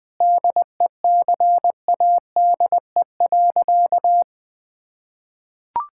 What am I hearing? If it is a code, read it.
Morse code, heard as a keyed tone, about 21 wpm, 700 Hz. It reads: DECADE.
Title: Sentences with 5-Word Phrases - Spoken Beginning - 20wpm